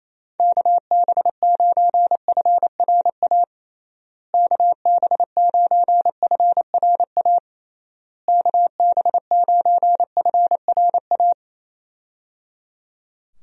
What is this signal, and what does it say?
Morse code, heard as a keyed tone, about 28 wpm, 700 Hz. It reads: KB9FRA KB9FRA KB9FRA
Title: USA Call Signs - 6 Characters - Repeated 3x - 28wpm